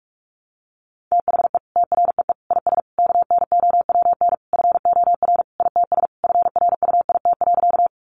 Morse code, heard as a keyed tone, about 46 wpm, 700 Hz. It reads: THE TREE IS KNOWN FOR ITS FRUIT.
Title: Sentences from Top 1300 Words 45wpm